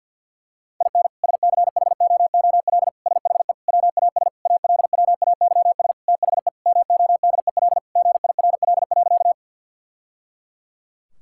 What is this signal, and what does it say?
Morse code, heard as a keyed tone, about 50 wpm, 700 Hz. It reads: IN SCHOOL SHE WAS ALWAYS THE MODEL GIRL.
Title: Sentences from Top 1000 Words - Review Part 5 - 50wpm